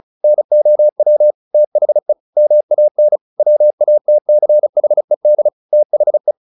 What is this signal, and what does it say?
Morse code, heard as a keyed tone, about 34 wpm, 600 Hz. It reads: NOW THE MAN WATCHED THE
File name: the old man and the sea / the old man and the sea 1 35wpm